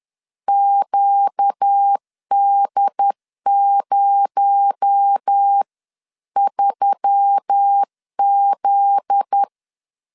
Morse code, heard as a keyed tone, about 10 wpm, 800 Hz. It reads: QD03Z
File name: LEZIONI / 7 GF9Z / 7 GF9Z 10wpm 2611